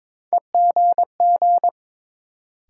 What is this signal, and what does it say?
Morse code, heard as a keyed tone, about 22 wpm, 700 Hz. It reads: EGG
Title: Sentences from Top 1000 Words - Review Part 7 - New Word Drill - 22wpm